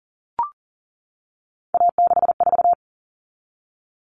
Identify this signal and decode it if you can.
Morse code, heard as a keyed tone, about 40 wpm, 700 Hz. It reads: A64
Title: All Cut Numbers - 3-Digit 40wpm